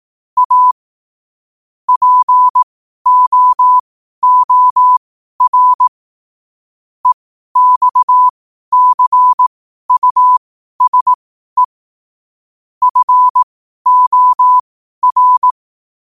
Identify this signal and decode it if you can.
Morse code, heard as a keyed tone, about 18 wpm, spaced as for 12 wpm, 1000 Hz. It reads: A POOR EXCUSE FOR